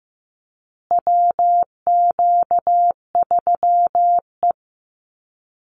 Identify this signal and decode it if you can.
Morse code, heard as a keyed tone, about 15 wpm, 700 Hz. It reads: WQ3E
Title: Call Signs Repeated 2x - 15wpm